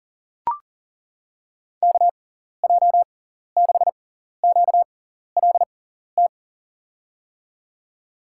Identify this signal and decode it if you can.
Morse code, heard as a keyed tone, about 40 wpm, spaced as for 14 wpm, 700 Hz. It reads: KJ6QLT
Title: USA Call Signs - 6x Character Spacing - 40wpm